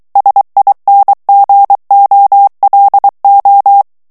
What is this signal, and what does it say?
Morse code, heard as a keyed tone, about 23 wpm, 800 Hz. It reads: SINGOLO
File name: ITALIEN texte clair 2 vitesse 22 mots minute